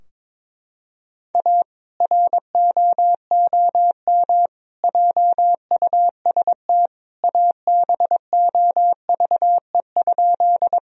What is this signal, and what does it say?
Morse code, heard as a keyed tone, about 22 wpm, 700 Hz. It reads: A ROOM JUST ABOVE?
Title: Sentences from Top 1000 Words - Review Part 2 - 22wpm